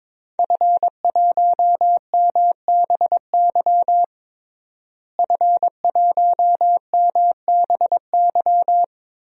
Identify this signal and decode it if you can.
Morse code, heard as a keyed tone, about 22 wpm, 700 Hz. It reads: F1MBY F1MBY
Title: DXCC-100 Call Signs - 5 Characters - Repeated 3x - 22wpm